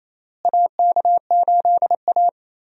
Morse code, heard as a keyed tone, about 28 wpm, 700 Hz. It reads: AK8A
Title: USA Call Signs - 4 Characters - Repeated 3x - 28wpm